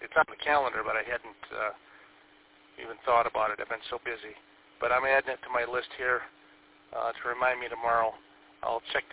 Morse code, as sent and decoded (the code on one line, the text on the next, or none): none